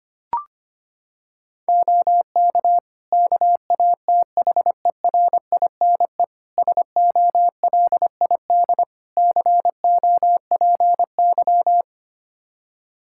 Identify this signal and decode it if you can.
Morse code, heard as a keyed tone, about 25 wpm, 700 Hz. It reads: OK KATHERINE SOLID COPY
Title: Practice QSO - Exchange 3 - More Info - 25wpm